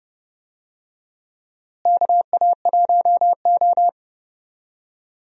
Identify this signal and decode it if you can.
Morse code, heard as a keyed tone, about 31 wpm, 700 Hz. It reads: KA1O
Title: USA Call Signs - 4 Characters - Repeated 3x - 30wpm